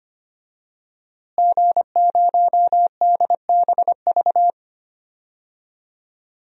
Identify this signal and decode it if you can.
Morse code, heard as a keyed tone, about 25 wpm, 700 Hz. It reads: G0DBV
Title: DXCC-100 Call Signs - 5 Characters - Repeated 2x - 25wpm